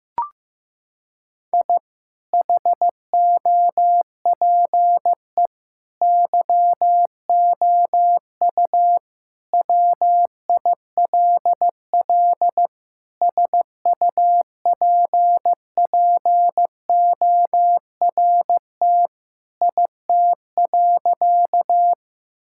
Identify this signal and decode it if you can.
Morse code, heard as a keyed tone, about 15 wpm, 700 Hz. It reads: I HOPE YOU WILL SUPPORT IT.
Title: Sentences from Top 1000 Words - Review Part 5 - 15wpm